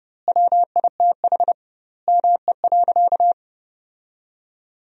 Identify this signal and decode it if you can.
Morse code, heard as a keyed tone, about 30 wpm, 700 Hz. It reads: WITH ME.
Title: 2x Word-Spacing - Sentences from Top 500 Words - 30wpm